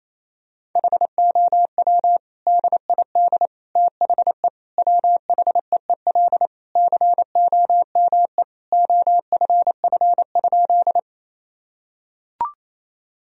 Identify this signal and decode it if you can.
Morse code, heard as a keyed tone, about 28 wpm, 700 Hz. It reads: HOW DID THE WHEEL COME OFF?
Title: Sentences from Top 1000 Words - Review Part 3 - 28wpm